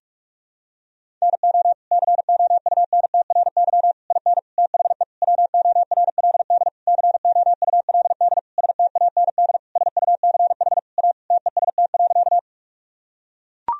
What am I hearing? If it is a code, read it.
Morse code, heard as a keyed tone, about 41 wpm, 700 Hz. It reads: NO COUNTRY IN THE WORLD COULD STAND SUCH A TEST.
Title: Sentences from Top 1000 Words - Review Part 3 - 45wpm